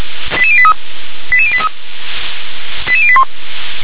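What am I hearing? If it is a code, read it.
Five-tone selective call (ZVEI), heard as quick runs of five tones, tones of about 80 ms. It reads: 88083 88083 88081
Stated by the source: ZVEI five-tone call received by SDR.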